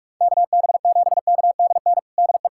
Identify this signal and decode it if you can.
Morse code, heard as a keyed tone, about 41 wpm, 700 Hz. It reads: KB7KDN DE
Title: USA Callsign DE Callsign - 6 Characters - 45wpm